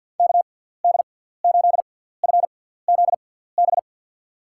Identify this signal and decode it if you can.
Morse code, heard as a keyed tone, about 46 wpm, spaced as for 17 wpm, 700 Hz. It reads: KD8FZB
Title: USA Call Signs - 6 Characters - 6x Character Spacing - 50wpm